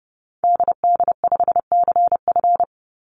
Morse code, heard as a keyed tone, about 30 wpm, 700 Hz. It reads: DD5CF
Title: Call Signs Repeated 2x - 30wpm